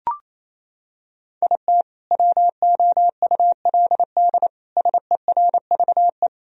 Morse code, heard as a keyed tone, about 28 wpm, 700 Hz. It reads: IT WOULD SERVE
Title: Sentences from Top 1000 Words - Review Part 2 - 28wpm